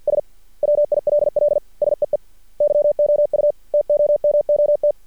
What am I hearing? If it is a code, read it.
Morse code, heard as a keyed tone, about 50 wpm, 590 Hz. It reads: S WILL SEE YOU TOMOT
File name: archived / CRSnet